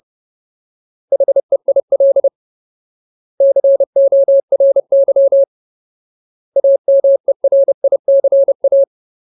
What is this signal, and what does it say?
Morse code, heard as a keyed tone, about 30 wpm, 550 Hz. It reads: HEIL CORY AMERICA
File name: Ultimate Word List / 30WPM Ultimate